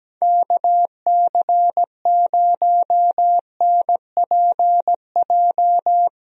Morse code, heard as a keyed tone, about 17 wpm, 700 Hz. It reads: KC0NPJ
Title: USA Call Signs - 6 Characters - Repeated 3x - 17wpm